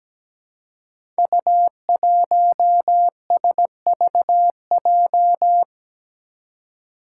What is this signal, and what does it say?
Morse code, heard as a keyed tone, about 17 wpm, 700 Hz. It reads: U1SVJ
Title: Five Challenging Letter-Number Combinations 17wpm